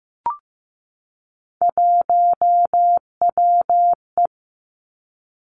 Morse code, heard as a keyed tone, about 15 wpm, 700 Hz. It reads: 1WE
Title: Three Challenging Letter-Number Combinations 15wpm